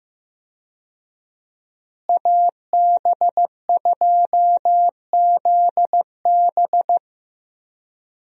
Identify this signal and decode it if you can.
Morse code, heard as a keyed tone, about 15 wpm, 700 Hz. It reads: AB2ZB
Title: USA Call Signs - 5 Characters - Repeated 3x - 15wpm